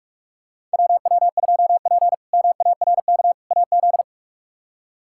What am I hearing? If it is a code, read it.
Morse code, heard as a keyed tone, about 43 wpm, 700 Hz. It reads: WW2P MARK AZ
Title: K1USN SST QSO Practice - 45wpm